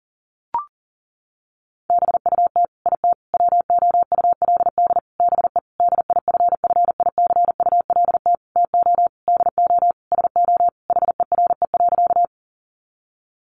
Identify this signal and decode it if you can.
Morse code, heard as a keyed tone, about 40 wpm, 700 Hz. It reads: BUT IT WOULD BE DIFFICULT TO DO SO HERE.
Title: Sentences from Top 600 Words 40wpm